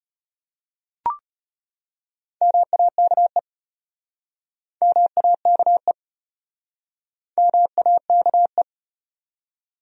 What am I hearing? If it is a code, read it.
Morse code, heard as a keyed tone, about 33 wpm, 700 Hz. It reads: MAKE MAKE MAKE